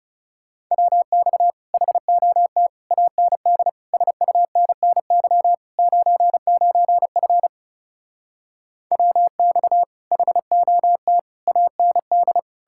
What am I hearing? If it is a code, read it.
Morse code, heard as a keyed tone, about 37 wpm, 700 Hz. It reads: WX HOT AND SUNNY 99F WX HOT AND
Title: Speed-Racing Common QSO Phrases Encore - 30wpm